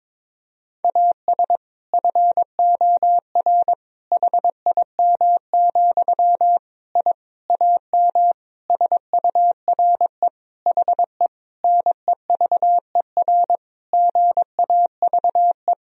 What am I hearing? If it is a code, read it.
Morse code, heard as a keyed tone, about 22 wpm, 700 Hz. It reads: AS FOR HIM, I AM SURE HE NEVER GAVE
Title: Sentences from Top 1000 Words - Review Part 2 - 22wpm